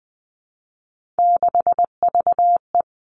Morse code, heard as a keyed tone, about 20 wpm, 700 Hz. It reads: 6VE
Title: Three Challenging Letter-Number Combinations 20wpm